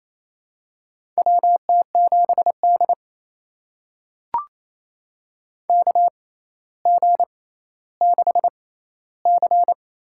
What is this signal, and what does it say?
Morse code, heard as a keyed tone, about 27 wpm, spaced as for 14 wpm, 700 Hz. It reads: WT7D KG6C
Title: USA Call Signs - 6x Character Spacing - 28wpm